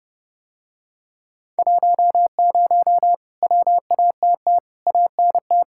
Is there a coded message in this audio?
Yes: Morse code from 1.6 to 5.6 s, heard as a keyed tone, about 30 wpm, 700 Hz. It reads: 10 WATT ANT